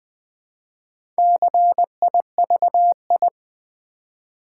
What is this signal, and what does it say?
Morse code, heard as a keyed tone, about 20 wpm, 700 Hz. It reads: CIVI
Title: Build A Word - 1 Letter At A Time - 3 to 5-Letter Words - 20wpm